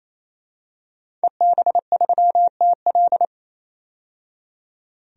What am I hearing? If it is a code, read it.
Morse code, heard as a keyed tone, about 28 wpm, 700 Hz. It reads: EB3TL
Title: DXCC-100 Call Signs - 5 Characters - 28wpm